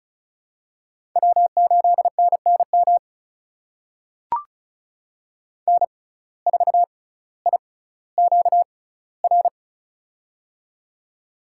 Morse code, heard as a keyed tone, about 34 wpm, spaced as for 18 wpm, 700 Hz. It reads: W8NNM N4IQR
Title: USA Call Signs - 5 Characters - 6x Character Spacing - 35wpm